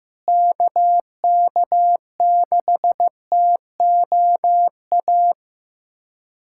Morse code, heard as a keyed tone, about 15 wpm, 700 Hz. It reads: KK6TOA
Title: USA Callsign DE Callsign - 6 Characters - 15wpm